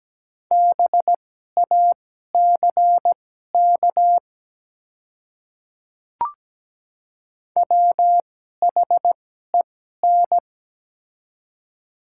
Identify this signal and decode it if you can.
Morse code, heard as a keyed tone, about 17 wpm, spaced as for 12 wpm, 690 Hz. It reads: BACK WHEN
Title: 2x Character Spacing - Top 300 Words - 17wpm